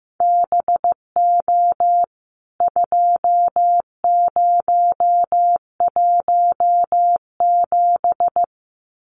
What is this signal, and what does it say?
Morse code, heard as a keyed tone, about 15 wpm, 700 Hz. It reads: BO 2017
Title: CWT Exchange 15wpm